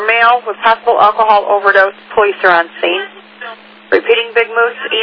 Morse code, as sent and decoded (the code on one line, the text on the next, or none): none